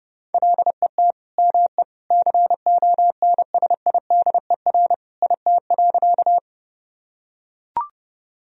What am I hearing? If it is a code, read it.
Morse code, heard as a keyed tone, about 32 wpm, 700 Hz. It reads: LET ME CONSIDER IT.
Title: Sentences from Top 1000 Words - Review Part 4 - 30wpm